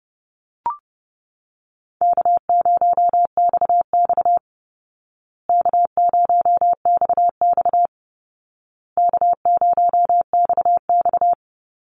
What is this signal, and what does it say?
Morse code, heard as a keyed tone, about 30 wpm, 700 Hz. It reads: K0XX K0XX K0XX
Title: Easy Call Signs Repeated 3x - 30wpm